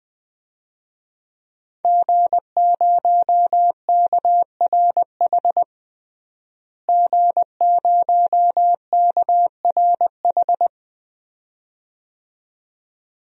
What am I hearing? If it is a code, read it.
Morse code, heard as a keyed tone, about 20 wpm, 700 Hz. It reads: G0KRH G0KRH
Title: DXCC-100 Call Signs - 5 Characters - Repeated 2x - 20wpm